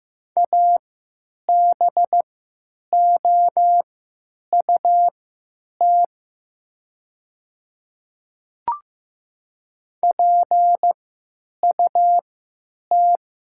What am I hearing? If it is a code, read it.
Morse code, heard as a keyed tone, about 15 wpm, spaced as for 9 wpm, 700 Hz. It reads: ABOUT PUT
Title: 3x Character Spacing - Top 200 Words - 15wpm